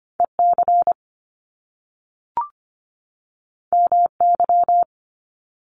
Morse code, heard as a keyed tone, about 25 wpm, 700 Hz. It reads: EC MY
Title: Top Two Letter Combinations in Words 25wpm